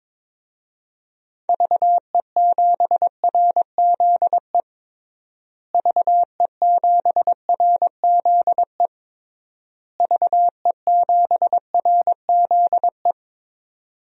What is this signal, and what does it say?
Morse code, heard as a keyed tone, about 22 wpm, 700 Hz. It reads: VE7RZE VE7RZE VE7RZE
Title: DXCC-100 Call Signs - 6 Characters - Repeated 3x - 22wpm